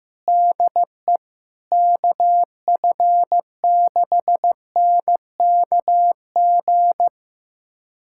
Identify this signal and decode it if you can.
Morse code, heard as a keyed tone, about 16 wpm, 700 Hz. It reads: DE KF6NKG
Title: USA Callsign DE Callsign - 6 Characters - 15wpm